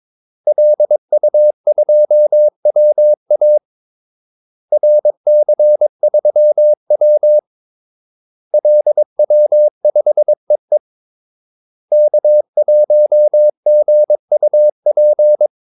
Morse code, heard as a keyed tone, about 22 wpm, 600 Hz. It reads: LU2WA RC3W LW5EE K1GUP
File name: Callsigns-22wpm